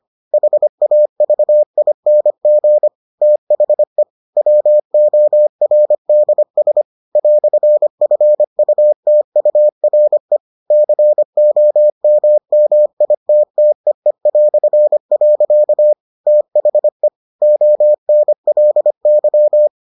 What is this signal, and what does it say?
Morse code, heard as a keyed tone, about 25 wpm, 600 Hz. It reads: HAVING THE WORDS "FUTURE COMMITTEE". THE ONLY